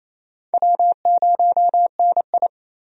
Morse code, heard as a keyed tone, about 27 wpm, 700 Hz. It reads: W0NI
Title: USA Call Signs - 4 Characters - 28wpm